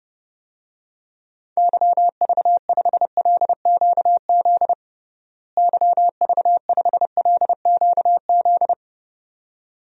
Morse code, heard as a keyed tone, about 30 wpm, 700 Hz. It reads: YV5LQZ YV5LQZ
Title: DXCC-100 Call Signs - 6 Characters - Repeated 2x - 30wpm